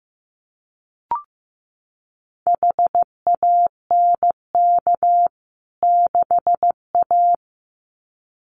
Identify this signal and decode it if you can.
Morse code, heard as a keyed tone, about 15 wpm, 700 Hz. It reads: HANK 6A
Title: CWT Exchange 15wpm